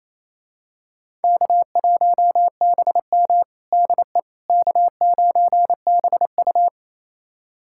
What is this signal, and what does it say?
Morse code, heard as a keyed tone, about 28 wpm, 700 Hz. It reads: K1BM DE K9BU
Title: USA Callsign DE Callsign - 4 Characters - 28wpm